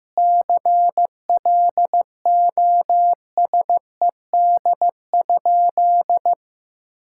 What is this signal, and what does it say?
Morse code, heard as a keyed tone, about 16 wpm, 700 Hz. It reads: CLOSED?
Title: Sentences from Top 1600 Words 15wpm